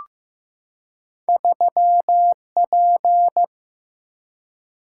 Morse code, heard as a keyed tone, about 15 wpm, 700 Hz. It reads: 3P
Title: Single-Letter Single-Number 15wpm